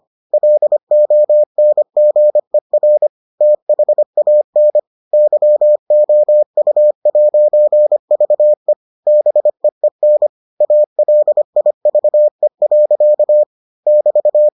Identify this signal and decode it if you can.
Morse code, heard as a keyed tone, about 25 wpm, 600 Hz. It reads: LONGER THAN YOU'VE BEEN ALIVE. <BT>